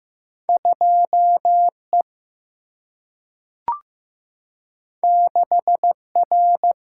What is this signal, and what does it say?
Morse code, heard as a keyed tone, about 15 wpm, 700 Hz. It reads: 2E 6R
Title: Single-Letter Single-Number 15wpm